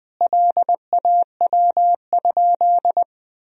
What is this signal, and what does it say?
Morse code, heard as a keyed tone, about 21 wpm, 700 Hz. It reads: LAW?
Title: Sentences from Top 1000 Words - Review Part 4 - 20wpm